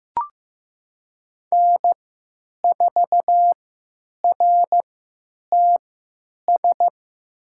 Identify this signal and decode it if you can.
Morse code, heard as a keyed tone, about 15 wpm, spaced as for 9 wpm, 700 Hz. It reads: N4RTS
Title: USA Call Signs - 5 Characters - 3x Character Spacing - 15wpm